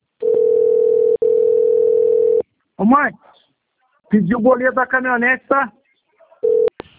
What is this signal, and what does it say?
Ringback tone (North America), 440 + 480 Hz, with speech.
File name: rg-1600-unknown- - - .16320